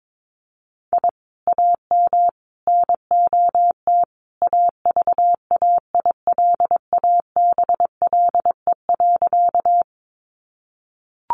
Morse code, heard as a keyed tone, about 22 wpm, 700 Hz. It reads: I AM NOT AVAILABLE.
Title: Sentences from Top 600 Words 22wpm